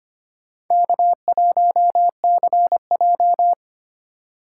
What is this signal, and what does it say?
Morse code, heard as a keyed tone, about 32 wpm, 700 Hz. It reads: K1CJ